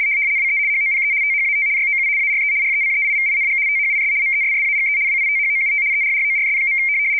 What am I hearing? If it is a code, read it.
RTTY, about 75 baud. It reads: WE HOPE THAT YOU ARE RECEIVING THIS MESSAGE WITHOUT ERRORS.
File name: rtty170 / rtty75.wav / rtty75